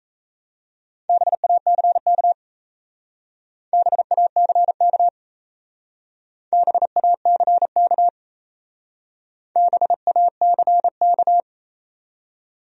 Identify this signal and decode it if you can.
Morse code, heard as a keyed tone, about 35 wpm, 700 Hz. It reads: BACK BACK BACK BACK
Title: Speed-Racing Top 200 Words 28wpm